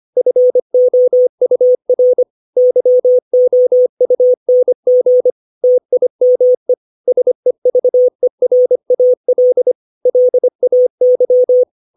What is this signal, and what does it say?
Morse code, heard as a keyed tone, about 25 wpm, 500 Hz. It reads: FOUR YOUNG TIME SEVERAL LAY
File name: 632 words / 632 Words 25WPM